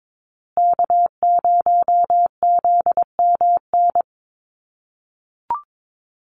Morse code, heard as a keyed tone, about 22 wpm, 700 Hz. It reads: K0ZMN
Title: USA Call Signs Repeated 2x 22wpm